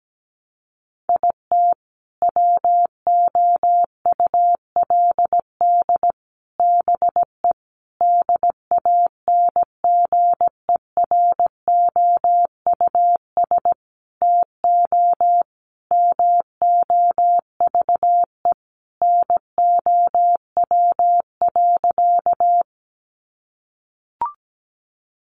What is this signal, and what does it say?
Morse code, heard as a keyed tone, about 17 wpm, 700 Hz. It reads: IT WOULD BE DANGEROUS TO MOVE NOW.
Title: Sentences from Top 1200 Words 17wpm